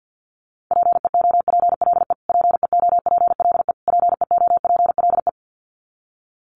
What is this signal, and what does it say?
Morse code, heard as a keyed tone, about 50 wpm, 700 Hz. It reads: PEOPLE PEOPLE PEOPLE
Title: Top 500 Words Repeated 50wpm